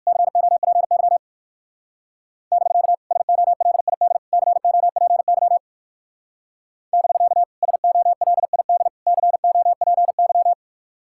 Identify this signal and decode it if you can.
Morse code, heard as a keyed tone, about 56 wpm, 700 Hz. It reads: COPY <BK> SOLID COPY <BK> SOLID COPY